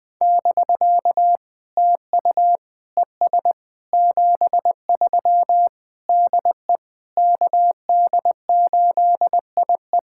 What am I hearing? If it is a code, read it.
Morse code, heard as a keyed tone, about 20 wpm, 700 Hz. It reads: <BK> TU ES 73 DE KD8IE
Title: POTA QSO Practice - 20wpm